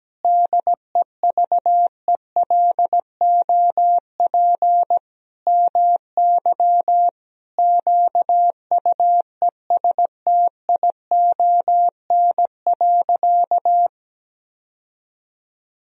Morse code, heard as a keyed tone, about 17 wpm, 700 Hz. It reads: DEVELOP MY QUESTION.
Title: Sentences from Top 1000 Words - Review Part 2 - 17wpm